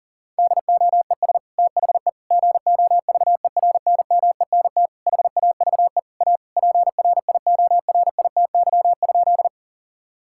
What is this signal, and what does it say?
Morse code, heard as a keyed tone, about 40 wpm, 700 Hz. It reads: DOES THE GOVERNMENT HAVE A PRIORITY?
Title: Sentences from Top 1600 Words 40wpm